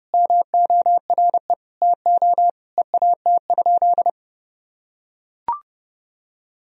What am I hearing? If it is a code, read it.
Morse code, heard as a keyed tone, about 30 wpm, 700 Hz. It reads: MORE TO EAT?
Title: Sentences from Top 300 Words and Abbreviations 30wpm